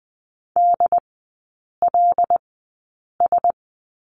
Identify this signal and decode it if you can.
Morse code, heard as a keyed tone, about 20 wpm, spaced as for 8 wpm, 700 Hz. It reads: DLS
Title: 114 Lesson 19e - Letter w - Three Characters 20wpm